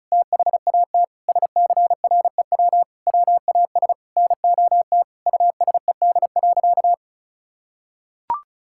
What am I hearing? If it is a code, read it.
Morse code, heard as a keyed tone, about 34 wpm, 700 Hz. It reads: THAT SCREW WAS NOT USED.
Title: Sentences from Top 1600 Words 35wpm